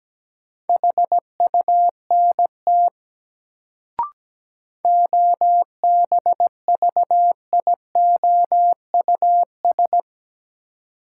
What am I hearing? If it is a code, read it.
Morse code, heard as a keyed tone, about 17 wpm, 700 Hz. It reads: HUNT OBVIOUS
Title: Sentences from Top 1600 Words - New Word Drill Alt - 17wpm